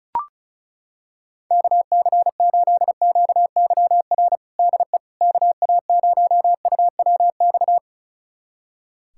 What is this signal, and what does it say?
Morse code, heard as a keyed tone, about 36 wpm, 700 Hz. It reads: KC8QYR DE KA0UWX
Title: USA Callsign DE Callsign - 6 Characters - 35wpm